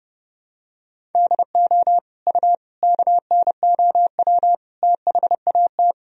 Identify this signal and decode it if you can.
Morse code, heard as a keyed tone, about 30 wpm, 700 Hz. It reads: DO U KNOW THAT
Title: Sentences from Top 400 Words and Abbreviations 30wpm